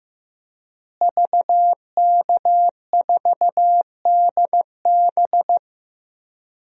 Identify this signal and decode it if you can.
Morse code, heard as a keyed tone, about 15 wpm, 700 Hz. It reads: VK4DB
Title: DXCC-100 Call Signs - 5 Characters - Repeated 3x - 15wpm